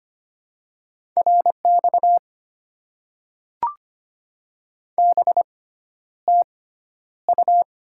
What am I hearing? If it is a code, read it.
Morse code, heard as a keyed tone, about 25 wpm, spaced as for 11 wpm, 700 Hz. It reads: RX BTU
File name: Top 90 QSO Elements - 6x Character Spacing - 25wpm